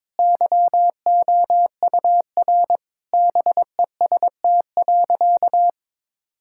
Morse code, heard as a keyed tone, about 22 wpm, 700 Hz. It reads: YOUR BEST.
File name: Sentences with 7-Word Phrases - 22wpm